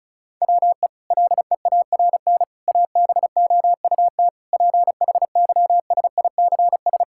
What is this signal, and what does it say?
Morse code, heard as a keyed tone, about 36 wpm, 700 Hz. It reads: WE LEARN ABOUT PHYSICS
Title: Sentences from Top 1000 Words - Review Part 5 - 35wpm